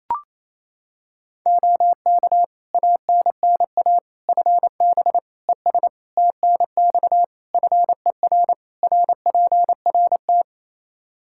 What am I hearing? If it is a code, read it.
Morse code, heard as a keyed tone, about 28 wpm, 700 Hz. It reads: OK ANNA FB ES TNX FER RPRT
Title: Practice QSO - Exchange 2 - Info - 28wpm